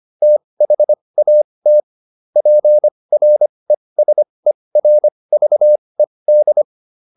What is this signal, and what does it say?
Morse code, heard as a keyed tone, about 25 wpm, spaced as for 20 wpm, 600 Hz. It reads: THAT PRESERVED